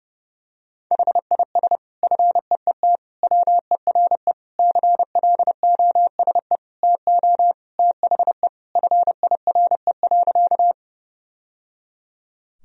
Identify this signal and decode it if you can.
Morse code, heard as a keyed tone, about 30 wpm, 700 Hz. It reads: HIS FEET WERE CLOSE TO THE FIRE.
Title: Sentences from Top 1000 Words - Review Part 1 - 30wpm